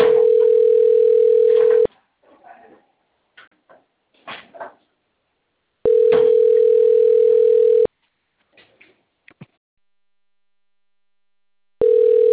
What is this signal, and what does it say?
Ringback tone (North America), 440 + 480 Hz.
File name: rg-1600-unknown- - - .88009